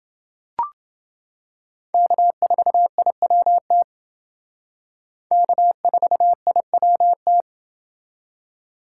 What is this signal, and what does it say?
Morse code, heard as a keyed tone, about 29 wpm, 700 Hz. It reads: K4IWT K4IWT